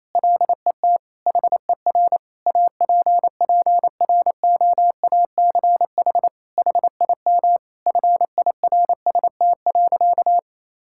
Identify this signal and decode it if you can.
Morse code, heard as a keyed tone, about 28 wpm, 700 Hz. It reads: LET HER APPROACH HIM FIRST.
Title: Sentences from Top 1000 Words - Review Part 6 - 28wpm